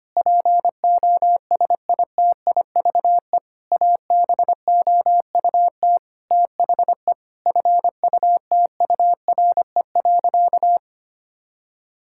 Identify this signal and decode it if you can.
Morse code, heard as a keyed tone, about 25 wpm, 700 Hz. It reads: POSITIVE ABOUT THE FUTURE.
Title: Sentences from Top 1000 Words - Review Part 5 - 25wpm